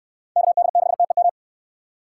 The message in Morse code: KD6TEM

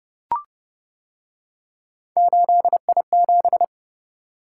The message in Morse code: 8I7

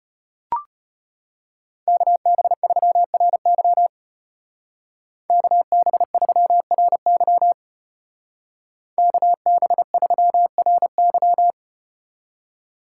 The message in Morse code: KB3RY KB3RY KB3RY

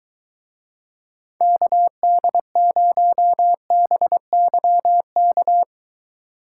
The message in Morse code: KD0BYK

